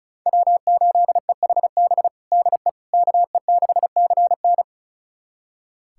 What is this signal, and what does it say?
Morse code, heard as a keyed tone, about 36 wpm, 700 Hz. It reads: W8EHB DE KE6CN
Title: USA Callsign DE Callsign - 5 Characters - 35wpm